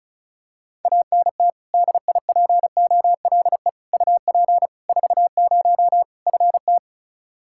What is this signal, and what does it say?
Morse code, heard as a keyed tone, about 36 wpm, 700 Hz. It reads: ANT DIPOLE UP 40 FT
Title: Practice QSO - Exchange 2 - Info - 35wpm